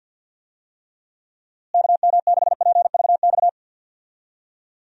Morse code, heard as a keyed tone, about 50 wpm, 700 Hz. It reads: KM6PVX
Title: USA Call Signs - 6 Characters - Repeated 3x - 50wpm